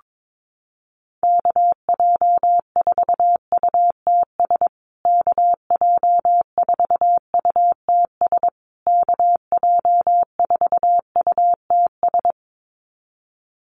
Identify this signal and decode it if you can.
Morse code, heard as a keyed tone, about 22 wpm, 700 Hz. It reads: KJ4UTS KJ4UTS KJ4UTS
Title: USA Call Signs Repeated 3x 22wpm